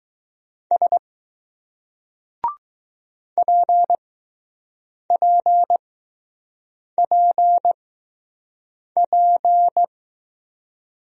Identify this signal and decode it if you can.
Morse code, heard as a keyed tone, about 20 wpm, spaced as for 6 wpm, 700 Hz. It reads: S PPPP